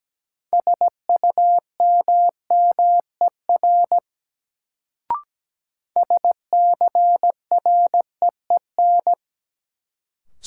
Morse code, heard as a keyed tone, about 17 wpm, 700 Hz. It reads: SUMMER SCREEN
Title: Sentences from Top 1000 Words - Review Part 6 - New Word Drill - 17wpm